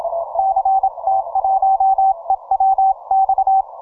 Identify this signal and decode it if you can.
Morse code, heard as a keyed tone, about 27 wpm, 770 Hz. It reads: CT1EWX
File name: 10MCW